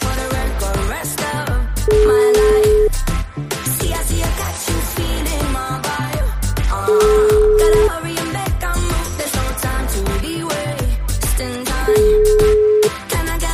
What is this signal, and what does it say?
Ringback tone (Europe, 425 Hz), with noise.